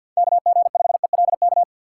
Morse code, heard as a keyed tone, about 53 wpm, 700 Hz. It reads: KG5ELK
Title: USA Call Signs - 6x Character Spacing - 50wpm